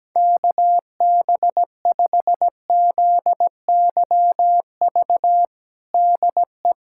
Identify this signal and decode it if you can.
Morse code, heard as a keyed tone, about 17 wpm, 700 Hz. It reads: KB5ZYV DE
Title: USA Callsign DE Callsign - 6 Characters - 17wpm